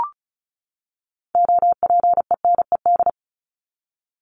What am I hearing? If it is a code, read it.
Morse code, heard as a keyed tone, about 34 wpm, 700 Hz. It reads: OPENED
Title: ED Suffix - Top 100 Words with Base Words 35wpm